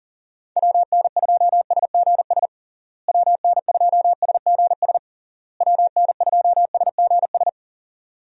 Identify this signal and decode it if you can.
Morse code, heard as a keyed tone, about 40 wpm, 700 Hz. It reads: WN2SGS WN2SGS WN2SGS
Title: USA Call Signs - 6 Characters - Repeated 3x - 40wpm